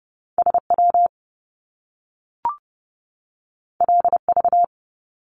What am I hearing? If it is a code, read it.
Morse code, heard as a keyed tone, about 30 wpm, 700 Hz. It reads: SW LV
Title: Top Two Letter Combinations in Words 30wpm